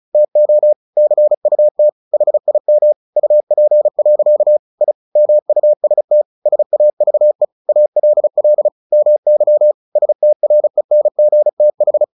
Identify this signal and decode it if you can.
Morse code, heard as a keyed tone, about 34 wpm, 600 Hz. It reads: TO CUT HIM UP. I MUST SAVE ALL MY STRENGTH
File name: the old man and the sea / the old man and the sea 1 35wpm